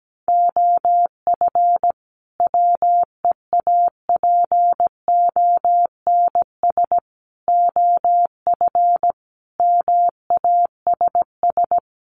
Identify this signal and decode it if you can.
Morse code, heard as a keyed tone, about 17 wpm, 700 Hz. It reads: OF WEAPONS OF MASS
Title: Sets of 5 Words Encore 17wpm